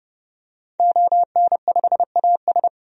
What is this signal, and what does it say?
Morse code, heard as a keyed tone, about 30 wpm, 700 Hz. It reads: ON5AS
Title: DXCC-100 Call Signs - 5 Characters - 30wpm